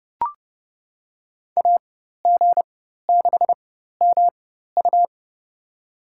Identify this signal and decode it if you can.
Morse code, heard as a keyed tone, about 30 wpm, spaced as for 14 wpm, 700 Hz. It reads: AG6MU
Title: USA Call Signs - 5 Characters - 4x Character Spacing - 30wpm